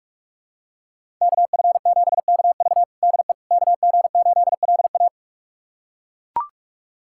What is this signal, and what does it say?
Morse code, heard as a keyed tone, about 42 wpm, 700 Hz. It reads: KF7KV DE KG8LA
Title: USA Callsign DE Callsign - 5 Characters - 45wpm